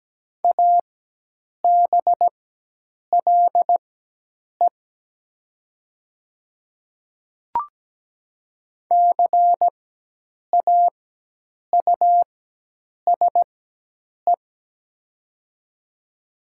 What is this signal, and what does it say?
Morse code, heard as a keyed tone, about 17 wpm, spaced as for 8 wpm, 700 Hz. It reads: ABLE CAUSE
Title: 4x Character Spacing - Top 500 Words - 17wpm